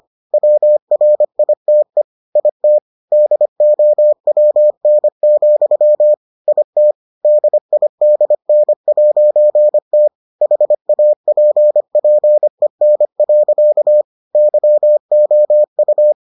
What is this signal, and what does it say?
Morse code, heard as a keyed tone, about 25 wpm, 600 Hz. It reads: WRITE IT DOWN, IT DIDN'T HAPPEN. YOU